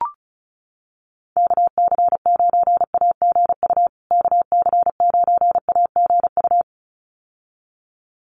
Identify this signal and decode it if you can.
Morse code, heard as a keyed tone, about 34 wpm, 700 Hz. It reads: KC9AGU KC9AGU
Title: USA Call Signs Repeated 2x 35wpm